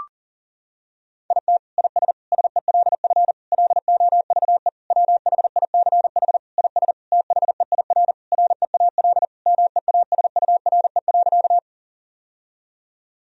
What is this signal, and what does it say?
Morse code, heard as a keyed tone, about 40 wpm, 700 Hz. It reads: IT IS SELF LOVE WHICH IS THEIR REAL MEASURE.
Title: Sentences from Top 1000 Words - Review Part 1 - 40wpm